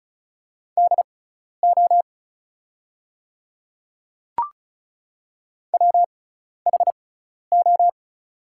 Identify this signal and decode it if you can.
Morse code, heard as a keyed tone, about 36 wpm, spaced as for 12 wpm, 700 Hz. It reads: DO WHO
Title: 6x Character Spacing - Top 100 Words - 35wpm